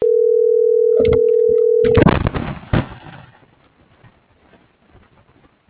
Ringback tone (North America), 440 + 480 Hz.